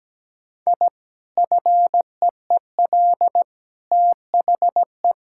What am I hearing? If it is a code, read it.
Morse code, heard as a keyed tone, about 17 wpm, 700 Hz. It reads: I FEEL THE
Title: Sentences from Top 1000 Words - Review Part 6 - 17wpm